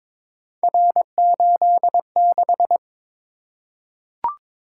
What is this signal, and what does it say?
Morse code, heard as a keyed tone, about 23 wpm, 700 Hz. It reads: R86
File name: practice / 22wpm / Three Letter-Number Combinations - 22wpm